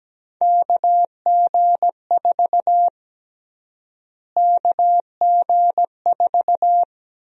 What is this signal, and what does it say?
Morse code, heard as a keyed tone, about 17 wpm, 700 Hz. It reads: KG4 KG4
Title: USA Call Sign Prefixes - Repeated 2x - 17wpm